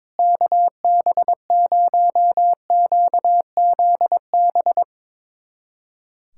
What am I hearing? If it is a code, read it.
Morse code, heard as a keyed tone, about 22 wpm, 700 Hz. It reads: KB0QZB
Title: USA Callsign DE Callsign 22wpm